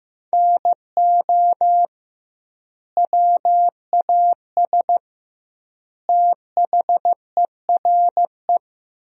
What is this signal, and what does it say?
Morse code, heard as a keyed tone, about 15 wpm, 700 Hz. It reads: NO WAS THERE
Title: 2x Word-Spacing - Sentences from Top 100 Words - 15wpm